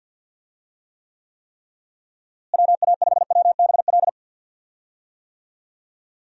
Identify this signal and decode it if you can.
Morse code, heard as a keyed tone, about 48 wpm, 700 Hz. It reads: WA5WBL
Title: USA Call Signs - 6 Characters - Repeated 3x - 50wpm